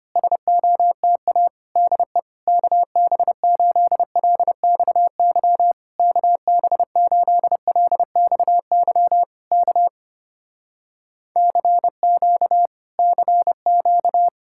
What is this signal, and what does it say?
Morse code, heard as a keyed tone, about 24 wpm, 700 Hz. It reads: SOTA DE KB8LXY KB8LXY K CQ CQ
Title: Speed-Racing Common QSO Phrases - 25wpm